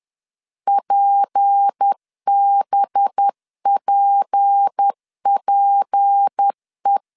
Morse code, heard as a keyed tone, about 10 wpm, 800 Hz. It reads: PBPPE
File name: LEZIONI / 6 PKX0 / 6 P 10wpm 2432